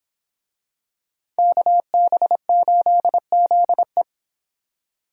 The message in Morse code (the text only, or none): KB8ZE